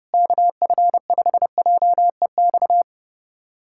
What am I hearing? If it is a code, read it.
Morse code, heard as a keyed tone, about 30 wpm, 700 Hz. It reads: KF5JEX
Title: USA Callsign DE Callsign - 6 Characters - 30wpm